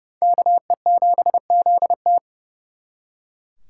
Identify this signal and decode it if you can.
Morse code, heard as a keyed tone, about 37 wpm, 700 Hz. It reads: KE7ZT
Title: Speed Racing - USA Call Signs 30wpm